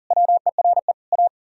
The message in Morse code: WERE A

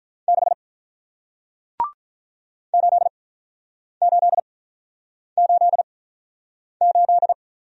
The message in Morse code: 6 8 8 8 8

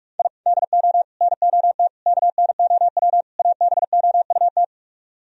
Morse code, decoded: I DO NOT KNOW ABOUT

